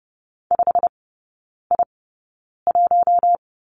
Morse code, heard as a keyed tone, about 30 wpm, spaced as for 9 wpm, 700 Hz. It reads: 5I1
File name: Lesson 19e - Letter w - Three Characters 30wpm